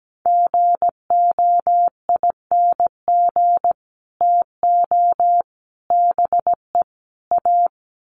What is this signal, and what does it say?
Morse code, heard as a keyed tone, about 17 wpm, 700 Hz. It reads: GOING TO BE A
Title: Sets of 4 Words 17wpm